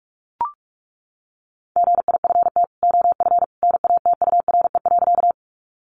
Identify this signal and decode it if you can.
Morse code, heard as a keyed tone, about 44 wpm, 700 Hz. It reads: GIFT OF NATURE.
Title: Sentences from Top 900 Words 45wpm